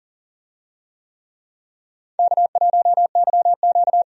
Morse code, heard as a keyed tone, about 40 wpm, 700 Hz. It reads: K1YQ
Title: USA Call Signs - 4 Characters - 40wpm